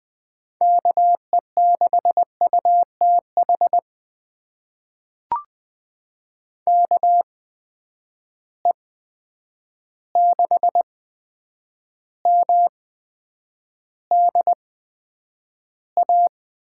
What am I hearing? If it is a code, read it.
Morse code, heard as a keyed tone, about 20 wpm, spaced as for 9 wpm, 700 Hz. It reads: KE6UTH KE6MDA
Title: USA Call Signs - 6 Characters - 8x Character Spacing - 20wpm